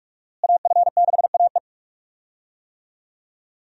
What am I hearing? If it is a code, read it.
Morse code, heard as a keyed tone, about 46 wpm, 700 Hz. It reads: AF6AE
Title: USA Call Signs - 5 Characters - Repeated 3x - 45wpm